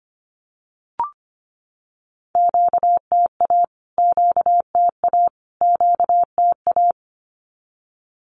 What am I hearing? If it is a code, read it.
Morse code, heard as a keyed tone, about 25 wpm, 700 Hz. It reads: QTA QTA QTA
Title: Q-Codes Repeated 3x 25wpm